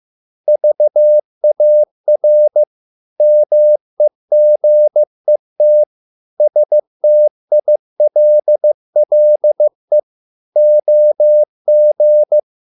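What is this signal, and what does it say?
Morse code, heard as a keyed tone, about 15 wpm, 600 Hz. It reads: VAR MEGET STILLE OG
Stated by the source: Morse code at a slow 15 wpm from CW med Gnister.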